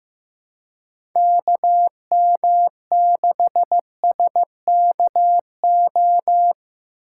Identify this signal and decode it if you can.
Morse code, heard as a keyed tone, about 15 wpm, 700 Hz. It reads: KM6SKO
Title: USA Call Signs - 6 Characters - 4x Character Spacing - 15wpm